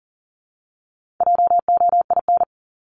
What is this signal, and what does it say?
Morse code, heard as a keyed tone, about 40 wpm, 700 Hz. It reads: JOIN
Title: Top 1251-1500 Words 40wpm